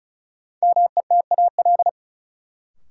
Morse code, heard as a keyed tone, about 36 wpm, 700 Hz. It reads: METAL
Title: Sentences from Top 1000 Words - Review Part 6 - New Word Drill - 35wpm